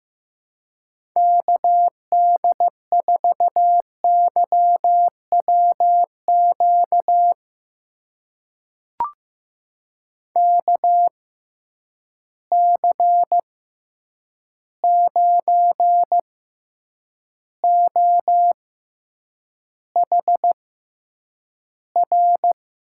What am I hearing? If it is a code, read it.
Morse code, heard as a keyed tone, about 15 wpm, spaced as for 8 wpm, 700 Hz. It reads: KD4YWQ KC9OHR
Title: USA Call Signs - 6 Characters - 6x Character Spacing - 15wpm